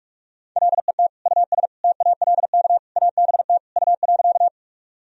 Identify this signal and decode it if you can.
Morse code, heard as a keyed tone, about 42 wpm, 700 Hz. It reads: LET US TALK ABT U.
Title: Sentences from Top 400 Words and Abbreviations 45wpm